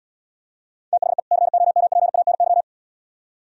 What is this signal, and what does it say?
Morse code, heard as a keyed tone, about 75 wpm, 700 Hz. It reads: THE COMPANY